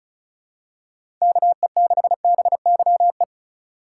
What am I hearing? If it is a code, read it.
Morse code, heard as a keyed tone, about 34 wpm, 700 Hz. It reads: KE6BYE
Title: USA Call Signs - 6 Characters - 2x Character Spacing - 35wpm